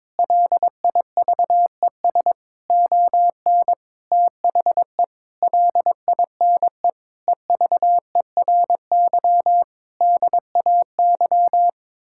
Morse code, heard as a keyed tone, about 22 wpm, 700 Hz. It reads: LIVES ON THE LINE EVERY DAY
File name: practice / 22wpm / Sets of 8 Words - 22wpm